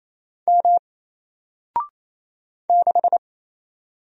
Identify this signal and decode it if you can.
Morse code, heard as a keyed tone, about 27 wpm, 700 Hz. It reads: M 6
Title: Single Letter-Number Alt 28wpm